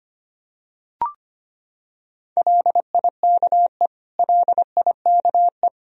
Morse code, heard as a keyed tone, about 25 wpm, 700 Hz. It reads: LIKE LIKE
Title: Top 100 Words Repeated 25wpm